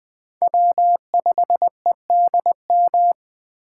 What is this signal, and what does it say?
Morse code, heard as a keyed tone, about 21 wpm, 700 Hz. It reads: W 5 E D M
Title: USA Call Signs - 5 Characters - 8x Character Spacing - 20wpm